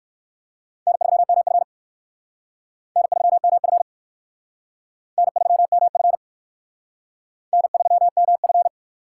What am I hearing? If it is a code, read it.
Morse code, heard as a keyed tone, about 58 wpm, 700 Hz. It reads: N3MF N3MF N3MF N3MF